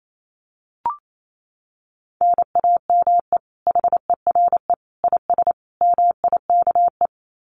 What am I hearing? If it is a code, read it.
Morse code, heard as a keyed tone, about 28 wpm, 700 Hz. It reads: NAME HERE IS MIKE
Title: Common QSO Phrases 28wpm